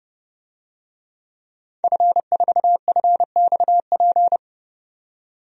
Morse code, heard as a keyed tone, about 30 wpm, 700 Hz. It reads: F4FXP
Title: DXCC-100 Call Signs - 5 Characters - 30wpm